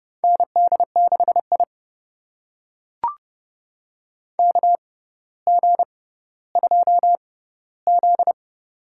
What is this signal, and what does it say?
Morse code, heard as a keyed tone, about 30 wpm, spaced as for 15 wpm, 700 Hz. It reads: ND6I KG2Z
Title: USA Call Signs - 4 Characters - 6x Character Spacing - 30wpm